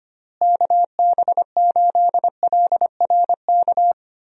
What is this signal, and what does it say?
Morse code, heard as a keyed tone, about 25 wpm, 700 Hz. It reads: KB8LRK
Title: USA Call Signs - 6 Characters - Repeated 3x - 25wpm